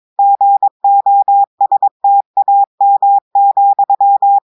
Morse code, heard as a keyed tone, about 22 wpm, 800 Hz. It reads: GOSTAM,
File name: audios / portugais / portugais texte en clair 2 vitesse 22 mots minutes